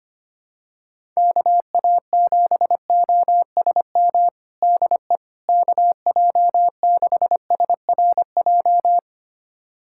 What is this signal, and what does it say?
Morse code, heard as a keyed tone, about 25 wpm, 700 Hz. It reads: KA7OSM DE KJ6SRJ
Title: USA Callsign DE Callsign 25wpm